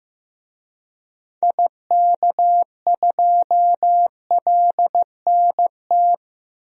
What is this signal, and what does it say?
Morse code, heard as a keyed tone, about 15 wpm, 700 Hz. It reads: IK2LNT
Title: DXCC-100 Call Signs - 6 Characters - 15wpm